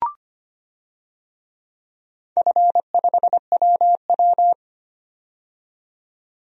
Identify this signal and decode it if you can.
Morse code, heard as a keyed tone, about 25 wpm, 700 Hz. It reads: F5WW
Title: DXCC-100 Call Signs - 4 Characters - 25wpm